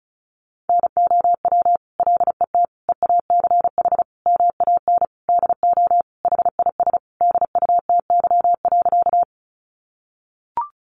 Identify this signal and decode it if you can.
Morse code, heard as a keyed tone, about 34 wpm, 700 Hz. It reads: NOW LET EACH MAN DO HIS DUTY.
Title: Sentences from Top 1500 Words 35wpm